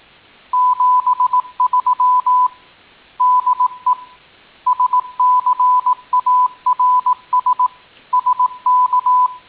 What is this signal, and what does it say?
Morse code, heard as a keyed tone, about 18 wpm, 1000 Hz. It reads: 73 DE SCARS SK
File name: tin-upgrade / QSO 2018 08 21